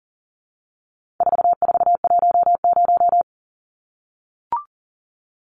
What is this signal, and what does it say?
Morse code, heard as a keyed tone, about 40 wpm, 700 Hz. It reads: 4410
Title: Numbers - 4-Digit 40wpm